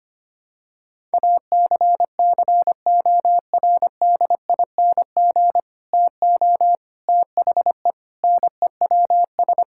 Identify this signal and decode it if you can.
Morse code, heard as a keyed tone, about 25 wpm, 700 Hz. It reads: ACCORDING TO THE NEWS
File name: practice / 25wpm / Sentences with 3-Word Phrases - 25wpm